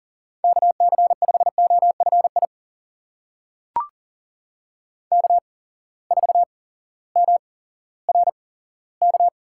Morse code, heard as a keyed tone, about 40 wpm, spaced as for 18 wpm, 700 Hz. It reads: KC5OFI K4MRK
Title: USA Call Signs - 8x Character Spacing - 40wpm